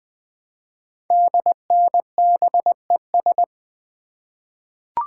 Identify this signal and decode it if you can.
Morse code, heard as a keyed tone, about 20 wpm, 700 Hz. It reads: DNBES
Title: Code Groups - 5 Letters - 20wpm